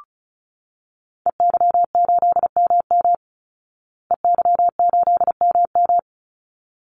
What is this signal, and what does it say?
Morse code, heard as a keyed tone, about 34 wpm, 700 Hz. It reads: EY8MM EY8MM
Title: Call Signs Repeated 3x - 35wpm